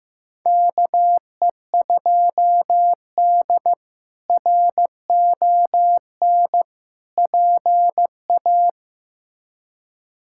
Morse code, heard as a keyed tone, about 16 wpm, 700 Hz. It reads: KE2D RON PA
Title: K1USN SST QSO Practice - 15wpm